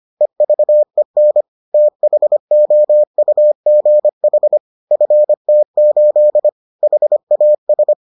Morse code, heard as a keyed tone, about 25 wpm, 600 Hz. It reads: EVEN THOUGH FT8 HAS